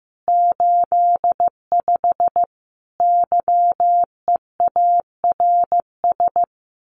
Morse code, heard as a keyed tone, about 15 wpm, 700 Hz. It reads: 85 YEARS
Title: Common QSO Phrases 15wpm